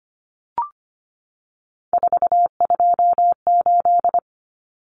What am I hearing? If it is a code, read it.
Morse code, heard as a keyed tone, about 25 wpm, 700 Hz. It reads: 428
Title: Common Cut Numbers - 3-Digit 25wpm